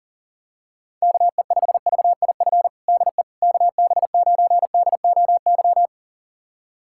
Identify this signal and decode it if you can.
Morse code, heard as a keyed tone, about 40 wpm, 700 Hz. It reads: KE5VIF DE KB9DOY
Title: USA Callsign DE Callsign - 6 Characters - 40wpm